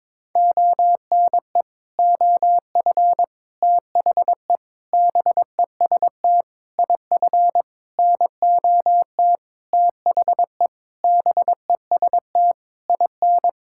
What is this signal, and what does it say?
Morse code, heard as a keyed tone, about 22 wpm, 700 Hz. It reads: ONE OF THE BEST IF NOT THE BEST IN
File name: practice / 22wpm / Sentences with 8-Word Phrases - 22wpm